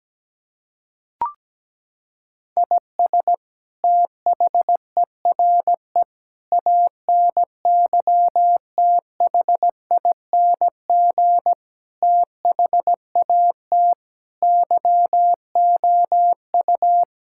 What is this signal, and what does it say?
Morse code, heard as a keyed tone, about 17 wpm, 700 Hz. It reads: IS THERE ANYTHING THAT YOU
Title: Sentences from Top 1000 Words - Review Part 6 - 17wpm